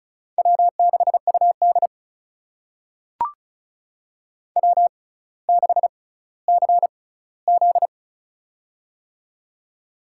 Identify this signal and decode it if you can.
Morse code, heard as a keyed tone, about 36 wpm, spaced as for 18 wpm, 700 Hz. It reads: W6UD W6CZ
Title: USA Call Signs - 6x Character Spacing - 35wpm